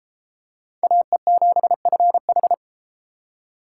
Morse code, heard as a keyed tone, about 31 wpm, 700 Hz. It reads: AE7FH